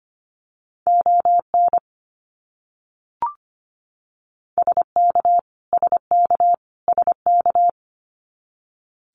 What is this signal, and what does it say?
Morse code, heard as a keyed tone, about 25 wpm, 700 Hz. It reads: ON SK SK SK
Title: Canadian Provinces and Territories - Repeated 3x - 25wpm